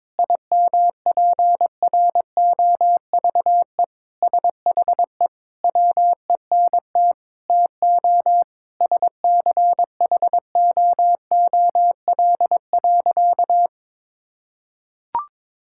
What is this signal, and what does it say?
Morse code, heard as a keyed tone, about 22 wpm, 700 Hz. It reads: IMPROVE SHE WENT TO SCHOOL.
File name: practice / 22wpm / Sentences with 4-Word Phrases - 22wpm